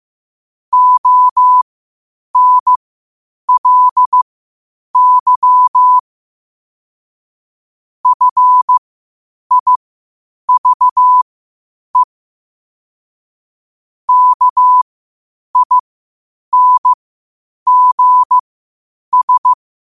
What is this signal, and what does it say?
Morse code, heard as a keyed tone, about 15 wpm, spaced as for 8 wpm, 1000 Hz. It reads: ONLY FIVE KINGS
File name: qotdfiles / 8wpm